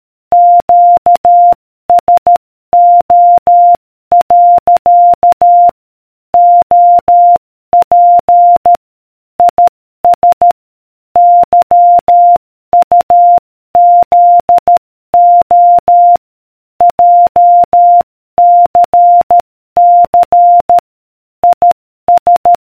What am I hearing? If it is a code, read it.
Morse code, heard as a keyed tone, about 13 wpm, 700 Hz. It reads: QSO. OP IS YUZO JCC IS